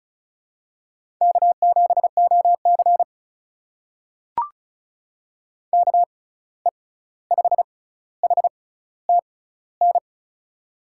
Morse code, heard as a keyed tone, about 36 wpm, spaced as for 16 wpm, 700 Hz. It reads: K7OC KE5HTN
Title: USA Call Signs - 6x Character Spacing - 35wpm